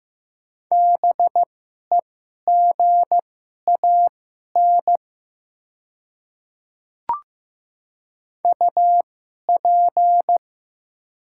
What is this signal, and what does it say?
Morse code, heard as a keyed tone, about 15 wpm, spaced as for 11 wpm, 700 Hz. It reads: BEGAN UP
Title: 2x Character Spacing - Top 500 Words - 15wpm